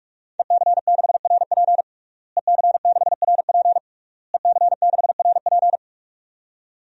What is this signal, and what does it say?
Morse code, heard as a keyed tone, about 46 wpm, 700 Hz. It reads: EC6RP EC6RP EC6RP
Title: DXCC-100 Call Signs - 5 Characters - Repeated 3x - 45wpm